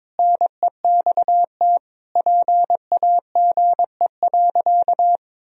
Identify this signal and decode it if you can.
Morse code, heard as a keyed tone, about 22 wpm, 700 Hz. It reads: NEXT PAGE.
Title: Sentences from Top 1000 Words - Review Part 2 - 22wpm